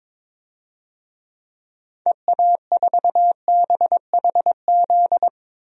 Morse code, heard as a keyed tone, about 22 wpm, 700 Hz. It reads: EA4BHZ
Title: DXCC-100 Call Signs - 6 Characters - 22wpm